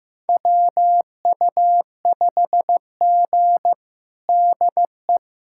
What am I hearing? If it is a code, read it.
Morse code, heard as a keyed tone, about 15 wpm, 700 Hz. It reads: WU5G DE
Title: USA Callsign DE Callsign - 4 Characters - 15wpm